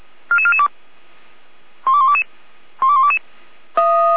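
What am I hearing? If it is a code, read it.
Five-tone selective call (ZVEI), heard as quick runs of five tones, tones of about 70 ms. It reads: 44442 12122 12122
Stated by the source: ZVEI five-tone selective call 44443 received on SDR.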